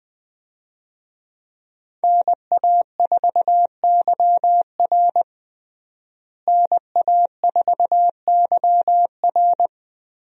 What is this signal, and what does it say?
Morse code, heard as a keyed tone, about 20 wpm, 700 Hz. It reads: NA4YR NA4YR
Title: USA Call Signs - 5 Characters - Repeated 3x - 20wpm